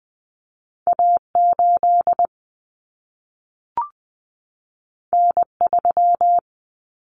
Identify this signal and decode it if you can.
Morse code, heard as a keyed tone, about 20 wpm, 700 Hz. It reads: A8 N3
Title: Common Cut Numbers - 2-Digit 20wpm